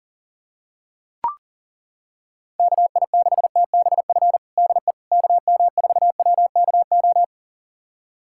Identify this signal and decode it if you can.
Morse code, heard as a keyed tone, about 37 wpm, 700 Hz. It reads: KI6TBF DE KM4WKO
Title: USA Callsign DE Callsign 40wpm